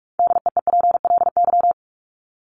Morse code, heard as a keyed tone, about 44 wpm, 700 Hz. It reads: DEEPLY
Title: Top 1751-2000 Words 45wpm